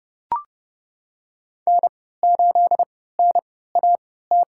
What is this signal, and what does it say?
Morse code, heard as a keyed tone, about 30 wpm, spaced as for 17 wpm, 700 Hz. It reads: N8NAT
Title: USA Call Signs - 5 Characters - 3x Character Spacing - 30wpm